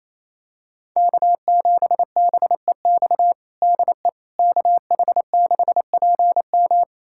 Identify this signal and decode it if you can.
Morse code, heard as a keyed tone, about 28 wpm, 700 Hz. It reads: K7BEX DE KH6PM
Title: USA Callsign DE Callsign - 5 Characters - 28wpm